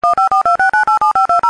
DTMF, heard as telephone touch tones, tones of about 120 ms. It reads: 15426987520